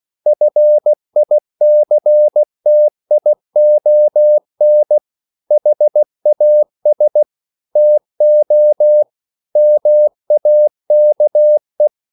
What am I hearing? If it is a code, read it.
Morse code, heard as a keyed tone, about 16 wpm, 600 Hz. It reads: FICTION HAS TO MAKE
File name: CW / Practice / EnglishQuotations0416